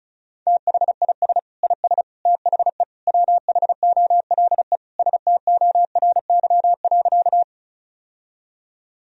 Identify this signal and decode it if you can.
Morse code, heard as a keyed tone, about 34 wpm, 700 Hz. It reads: THIS IS THE WHOLE STORY.
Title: Sentences from Top 1000 Words - Review Part 1 - 35wpm